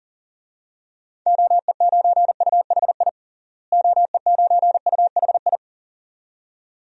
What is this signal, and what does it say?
Morse code, heard as a keyed tone, about 40 wpm, 700 Hz. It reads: OE9UHI OE9UHI
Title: DXCC-100 Call Signs - 6 Characters - Repeated 2x - 40wpm